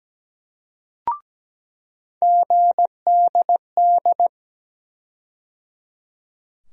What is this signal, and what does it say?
Morse code, heard as a keyed tone, about 17 wpm, 700 Hz. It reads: GDD
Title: USA Call Sign Postfixes - 3 Letters 17wpm